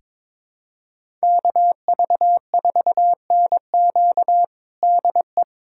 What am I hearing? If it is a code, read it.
Morse code, heard as a keyed tone, about 22 wpm, 700 Hz. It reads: KV4NQ DE
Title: USA Callsign DE Callsign 22wpm